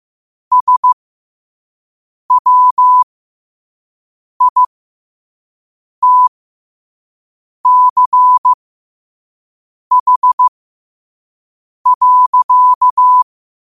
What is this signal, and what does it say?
Morse code, heard as a keyed tone, about 15 wpm, spaced as for 5 wpm, 1000 Hz. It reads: SWITCH.
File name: qotdfiles / 5wpm